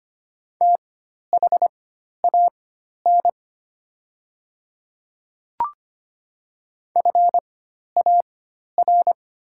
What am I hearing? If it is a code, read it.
Morse code, heard as a keyed tone, about 25 wpm, spaced as for 12 wpm, 700 Hz. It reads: THAN FAR
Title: 4x Character Spacing - Top 300 Words - 25wpm